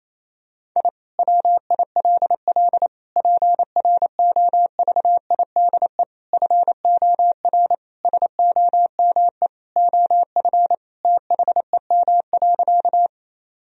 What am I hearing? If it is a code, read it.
Morse code, heard as a keyed tone, about 28 wpm, 700 Hz. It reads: I WILL PROVIDE FOR SOME OF THEM.
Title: Sentences from Top 1000 Words - Review Part 3 - 28wpm